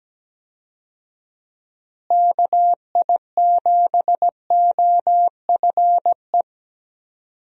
Morse code, heard as a keyed tone, about 17 wpm, 700 Hz. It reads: KI7OFE
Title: USA Call Signs - 6 Characters - Repeated 3x - 17wpm